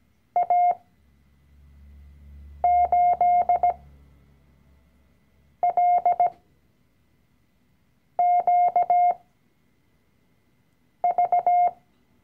Morse code, heard as a keyed tone, about 17 wpm, spaced as for 4 wpm, 710 Hz. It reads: A8LQV
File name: CW-Online-Kurs / Anfänger / Lektion 012